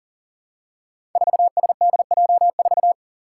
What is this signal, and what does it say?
Morse code, heard as a keyed tone, about 40 wpm, 700 Hz. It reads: 4SDJ4
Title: Five Challenging Letter-Number Combinations 40wpm